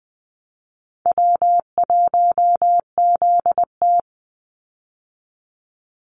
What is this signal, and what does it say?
Morse code, heard as a keyed tone, about 20 wpm, 700 Hz. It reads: W1ZT
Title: Easy Call Signs Repeated 2x - 20wpm